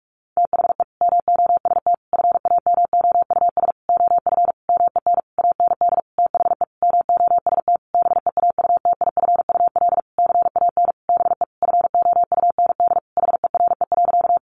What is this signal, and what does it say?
Morse code, heard as a keyed tone, about 44 wpm, 700 Hz. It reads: THE MOST FAMOUS OF MEN AND THE MOST BEAUTIFUL CAN BE FOUND HERE.
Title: Sentences from Top 1500 Words 45wpm